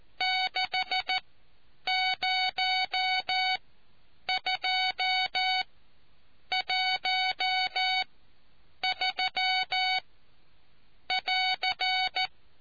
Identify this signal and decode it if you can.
Morse code, heard as a keyed tone, about 14 wpm, spaced as for 7 wpm, 750 Hz. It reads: 60213<AR>